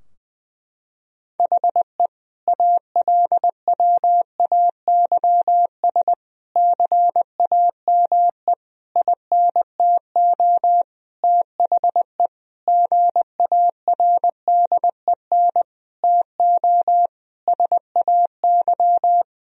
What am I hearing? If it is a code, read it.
Morse code, heard as a keyed tone, about 20 wpm, 700 Hz. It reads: HE ALWAYS CAME INTO THE GARDEN TO SAY
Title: Sentences from Top 1000 Words - Review Part 4 - 20wpm